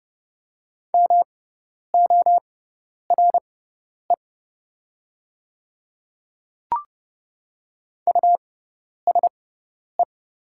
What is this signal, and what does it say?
Morse code, heard as a keyed tone, about 30 wpm, spaced as for 10 wpm, 700 Hz. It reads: MORE USE
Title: 6x Character Spacing - Top 100 Words - 30wpm